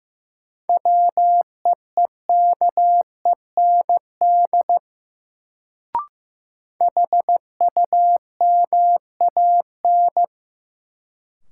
Morse code, heard as a keyed tone, about 15 wpm, 700 Hz. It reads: WEEKEND HUMAN
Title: Sets of 5 Words - New Word Drill - 15wpm